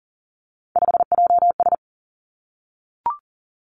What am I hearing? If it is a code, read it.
Morse code, heard as a keyed tone, about 42 wpm, 700 Hz. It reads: 5JS
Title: Three Challenging Letter-Number Combinations 40wpm